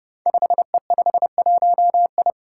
Morse code, heard as a keyed tone, about 28 wpm, 700 Hz. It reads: 5E51I
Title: Five Challenging Letter-Number Combinations 30wpm